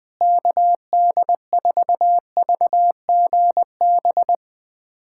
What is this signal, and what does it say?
Morse code, heard as a keyed tone, about 21 wpm, 700 Hz. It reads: KD4VGB
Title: USA Callsign DE Callsign 20wpm